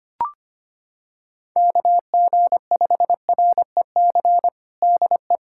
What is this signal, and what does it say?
Morse code, heard as a keyed tone, about 25 wpm, 700 Hz. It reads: KG5REC DE
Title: USA Callsign DE Callsign - 6 Characters - 25wpm